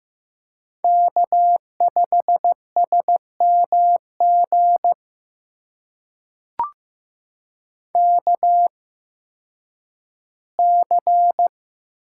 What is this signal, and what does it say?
Morse code, heard as a keyed tone, about 15 wpm, spaced as for 10 wpm, 700 Hz. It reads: K5SMG KC
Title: USA Call Signs - 5 Characters - 8x Character Spacing - 15wpm